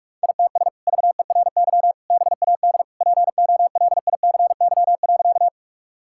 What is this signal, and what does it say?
Morse code, heard as a keyed tone, about 42 wpm, 700 Hz. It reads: ITS VERY BAD POLICY.
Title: Sentences from Top 1000 Words - Review Part 5 - 45wpm